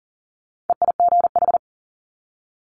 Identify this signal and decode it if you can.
Morse code, heard as a keyed tone, about 39 wpm, 700 Hz. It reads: EIGH
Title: Multi-letter Phonemes 40wpm